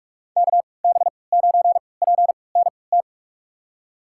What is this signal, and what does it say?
Morse code, heard as a keyed tone, about 46 wpm, spaced as for 26 wpm, 700 Hz. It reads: KB9PNT
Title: USA Call Signs - 6 Characters - 3x Character Spacing - 45wpm